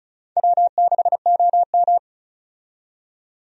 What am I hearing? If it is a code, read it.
Morse code, heard as a keyed tone, about 36 wpm, 700 Hz. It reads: W6OM
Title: USA Call Signs - 4 Characters - 35wpm